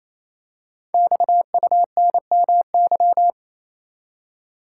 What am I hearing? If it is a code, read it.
Morse code, heard as a keyed tone, about 28 wpm, 700 Hz. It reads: XUNMY
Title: Code Groups - 5 Letters - 28wpm